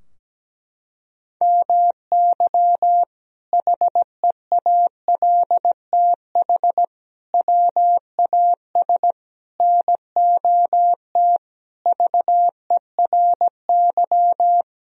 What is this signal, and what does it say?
Morse code, heard as a keyed tone, about 17 wpm, 700 Hz. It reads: MY HEALTH WAS NOT VERY
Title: Sentences from Top 1000 Words - Review Part 3 - 17wpm